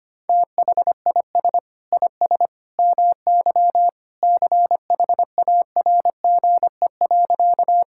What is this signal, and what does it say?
Morse code, heard as a keyed tone, about 25 wpm, 700 Hz. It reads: THIS IS MY CHARGE.
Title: Sentences from Top 1000 Words - Review Part 6 - 25wpm